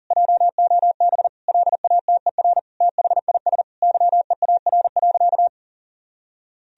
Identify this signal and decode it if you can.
Morse code, heard as a keyed tone, about 37 wpm, 700 Hz. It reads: JOB LATER THIS YEAR.
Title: Sentences from Top 1000 Words - Review Part 3 - 40wpm